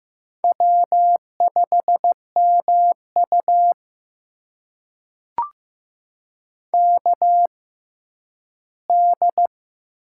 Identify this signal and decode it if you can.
Morse code, heard as a keyed tone, about 15 wpm, spaced as for 10 wpm, 700 Hz. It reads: W5MU KD
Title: USA Call Signs - 6x Character Spacing - 15wpm